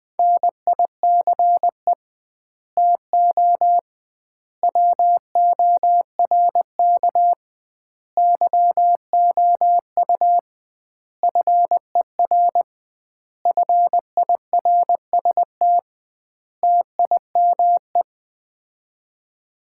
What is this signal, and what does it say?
Morse code, heard as a keyed tone, about 20 wpm, 700 Hz. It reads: NICE TO WORK YOU FER FIRST TIME
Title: Common QSO Phrases - 2x Word-Spacing - 20wpm